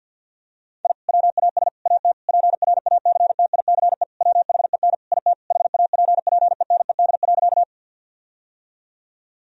Morse code, heard as a keyed tone, about 51 wpm, 700 Hz. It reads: I WAS AT PRACTICE WHEN IT HAPPENED.
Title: Sentences from Top 1000 Words - Review Part 4 - 50wpm